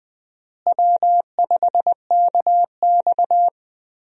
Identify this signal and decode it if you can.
Morse code, heard as a keyed tone, about 20 wpm, 700 Hz. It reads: W5KX
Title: USA Call Signs - 4 Characters - Repeated 3x - 20wpm